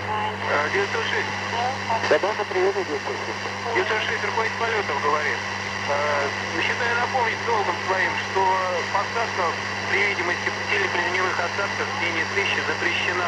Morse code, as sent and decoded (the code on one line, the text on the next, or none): none